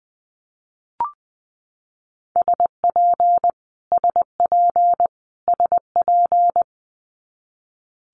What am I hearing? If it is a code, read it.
Morse code, heard as a keyed tone, about 20 wpm, 700 Hz. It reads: SP SP SP
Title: Multi-letter Phonemes Repeated 3x 20wpm